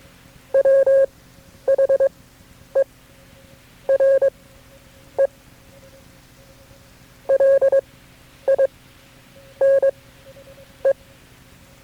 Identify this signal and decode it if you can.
Morse code, heard as a keyed tone, about 22 wpm, spaced as for 9 wpm, 540 Hz. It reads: WHERE LINE
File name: KB6MT-tape3-side1